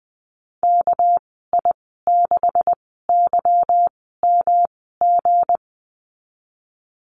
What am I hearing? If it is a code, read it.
Morse code, heard as a keyed tone, about 20 wpm, spaced as for 15 wpm, 700 Hz. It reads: KI6YMG
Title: 238 Lesson 38g - Letter q - Call Signs 20wpm